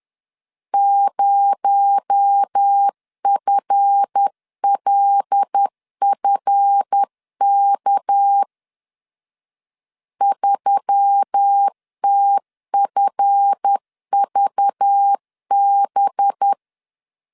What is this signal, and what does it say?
Morse code, heard as a keyed tone, about 10 wpm, 800 Hz. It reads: 0FLFK 3TFVB
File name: LEZIONI / 7 GF9Z / 7 GF 10wpm 2514